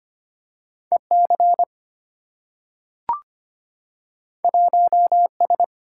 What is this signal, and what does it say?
Morse code, heard as a keyed tone, about 25 wpm, 700 Hz. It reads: EC 1S
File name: Two Letter-Number Combinations - 25wpm